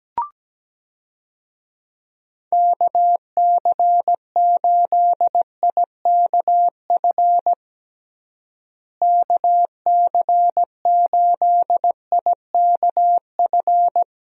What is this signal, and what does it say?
Morse code, heard as a keyed tone, about 17 wpm, 700 Hz. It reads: KC8IKF KC8IKF
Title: USA Call Signs - 6 Characters - Repeated 3x - 17wpm